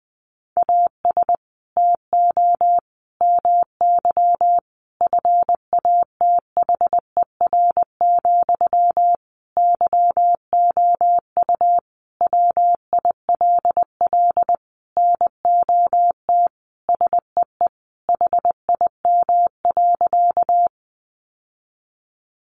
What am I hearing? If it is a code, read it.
Morse code, heard as a keyed tone, about 19 wpm, 700 Hz. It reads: AS TO MY FATHER, YOU WILL NOT SEE HIM.
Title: Sentences from Top 600 Words 20wpm